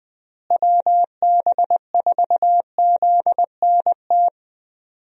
Morse code, heard as a keyed tone, about 20 wpm, 700 Hz. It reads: WB4ZNT
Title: USA Call Signs - 6 Characters - Repeated 2x - 20wpm